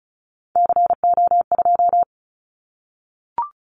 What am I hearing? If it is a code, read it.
Morse code, heard as a keyed tone, about 36 wpm, 700 Hz. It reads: CO2
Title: Callsign Prefixes for Top 100 DXCC - 35wpm